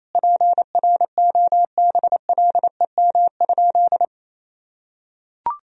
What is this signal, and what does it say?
Morse code, heard as a keyed tone, about 27 wpm, 700 Hz. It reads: PROBLEM?
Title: Sentences from Top 1000 Words - Review Part 3 - 28wpm